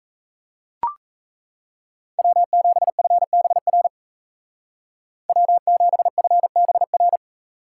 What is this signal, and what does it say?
Morse code, heard as a keyed tone, about 37 wpm, 700 Hz. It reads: W7FBR W7FBR